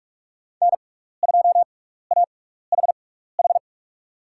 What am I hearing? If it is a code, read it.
Morse code, heard as a keyed tone, about 44 wpm, spaced as for 15 wpm, 700 Hz. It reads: N2AHH
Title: USA Call Signs - 5 Characters - 6x Character Spacing - 45wpm